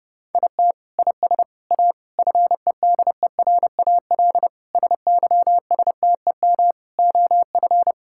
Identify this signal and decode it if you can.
Morse code, heard as a keyed tone, about 32 wpm, 700 Hz. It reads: IT IS A FEDERAL SYSTEM OF
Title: Sentences from Top 1000 Words - Review Part 5 - 30wpm